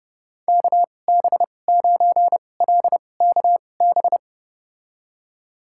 Morse code, heard as a keyed tone, about 29 wpm, spaced as for 22 wpm, 700 Hz. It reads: KB9LKB
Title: USA Call Signs - 6 Characters - 2x Character Spacing - 30wpm